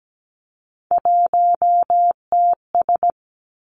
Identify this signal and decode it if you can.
Morse code, heard as a keyed tone, about 17 wpm, 700 Hz. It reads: 1TS
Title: Three Challenging Letter-Number Combinations 17wpm